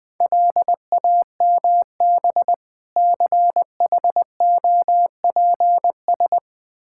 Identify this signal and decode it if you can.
Morse code, heard as a keyed tone, about 20 wpm, 700 Hz. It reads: LAMB CHOPS
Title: Difficult Words with Hints - 20wpm